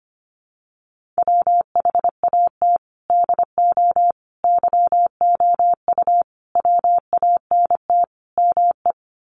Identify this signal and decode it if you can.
Morse code, heard as a keyed tone, about 25 wpm, 700 Hz. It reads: WHAT DO YOU WANT ME
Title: Sets of 5 Words Encore2 25wpm